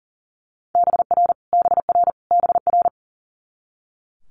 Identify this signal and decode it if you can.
Morse code, heard as a keyed tone, about 40 wpm, 690 Hz. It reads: BR BR BR
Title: Multi-letter Phonemes Repeated 3x 40wpm